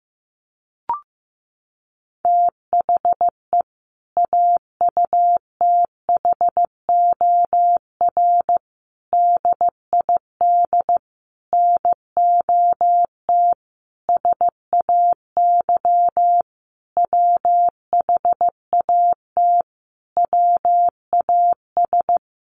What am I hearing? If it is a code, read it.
Morse code, heard as a keyed tone, about 15 wpm, 700 Hz. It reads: THE AUTHOR DID NOT SAY WHAT WAS
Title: Sentences from Top 1200 Words 15wpm